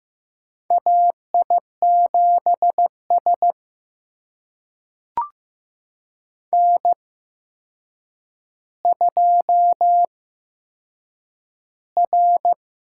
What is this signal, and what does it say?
Morse code, heard as a keyed tone, about 15 wpm, spaced as for 7 wpm, 700 Hz. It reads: AI7S N2R
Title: USA Call Signs - 8x Character Spacing - 15wpm